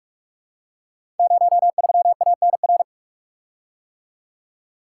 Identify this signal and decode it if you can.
Morse code, heard as a keyed tone, about 44 wpm, 700 Hz. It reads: 03ANR
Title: Code Groups - 5 Letter:Numbers - 45wpm